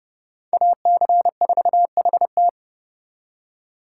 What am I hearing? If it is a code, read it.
Morse code, heard as a keyed tone, about 38 wpm, 700 Hz. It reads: AC4HT